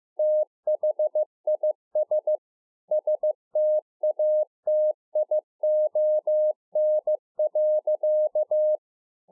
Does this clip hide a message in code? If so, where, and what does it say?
Yes: Morse code from 0.2 to 8.8 s, heard as a keyed tone, about 15 wpm, 610 Hz. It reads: THIS STATION.